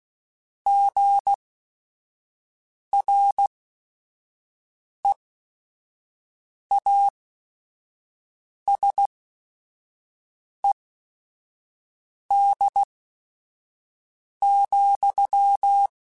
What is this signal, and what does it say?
Morse code, heard as a keyed tone, about 16 wpm, spaced as for 5 wpm, 800 Hz. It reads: GREASED,